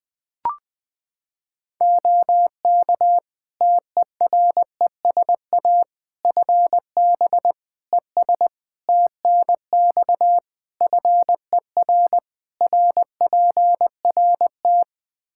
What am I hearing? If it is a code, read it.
Morse code, heard as a keyed tone, about 20 wpm, 700 Hz. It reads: OK TERESA FB ES TNX FER RPRT
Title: Practice QSO - Exchange 2 - Info - 20wpm